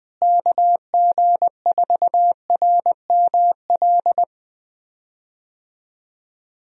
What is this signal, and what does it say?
Morse code, heard as a keyed tone, about 20 wpm, 700 Hz. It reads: KG4RML
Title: USA Call Signs - 6 Characters - Repeated 2x - 20wpm